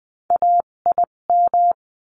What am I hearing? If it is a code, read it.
Morse code, heard as a keyed tone, about 20 wpm, spaced as for 17 wpm, 700 Hz. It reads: AIM
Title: 257 Lesson 42b - Words 14-20wpm